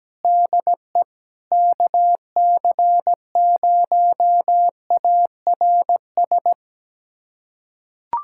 Morse code, heard as a keyed tone, about 17 wpm, 700 Hz. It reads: DE KC0ARS
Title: USA Callsign DE Callsign - 6 Characters - 17wpm